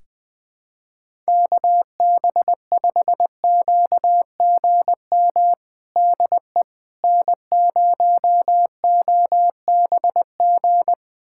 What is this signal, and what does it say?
Morse code, heard as a keyed tone, about 20 wpm, 700 Hz. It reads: KB5QGM DE N0OBG
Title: USA Callsign DE Callsign 20wpm